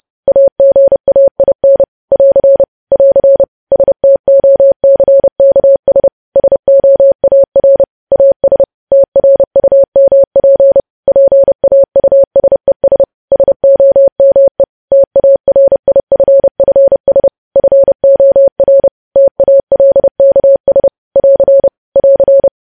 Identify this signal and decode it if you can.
Morse code, heard as a keyed tone, about 30 wpm, 550 Hz. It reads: AGAIN <AR> <AR> STOCKS SOAR AS TRUMP PAUSES SOME TARIFFS FOR TALKS <AR> <AR>
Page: news headlines in morse code at 30 wpm